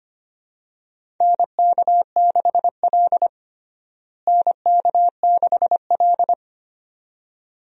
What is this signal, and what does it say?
Morse code, heard as a keyed tone, about 25 wpm, 700 Hz. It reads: NK6L NK6L
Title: USA Call Signs - 4 Characters - Repeated 2x - 25wpm